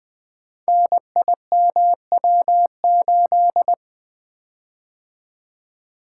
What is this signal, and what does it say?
Morse code, heard as a keyed tone, about 20 wpm, 700 Hz. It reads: NIMW8
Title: Code Groups - 5 Letter:Numbers - 20wpm